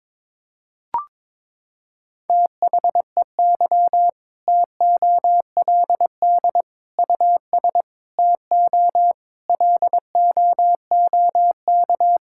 Morse code, heard as a keyed tone, about 22 wpm, 700 Hz. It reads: THEY TOLD US TO LOOK
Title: Sentences from Top 1600 Words 22wpm